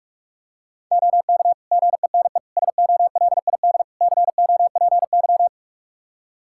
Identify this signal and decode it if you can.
Morse code, heard as a keyed tone, about 46 wpm, 700 Hz. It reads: OK GENE SOLID COPY
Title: Practice QSO - Exchange 3 - More Info - 45wpm